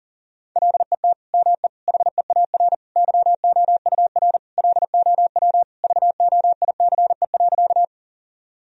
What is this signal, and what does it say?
Morse code, heard as a keyed tone, about 40 wpm, 700 Hz. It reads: LET ME HEAR YOUR LOW VOICE.
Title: Sentences from Top 1000 Words - Review Part 1 - 40wpm